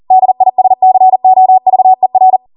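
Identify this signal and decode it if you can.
Morse code, heard as a keyed tone, about 40 wpm, 750 Hz. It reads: DISCOVER
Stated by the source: Code practice at 40 wpm.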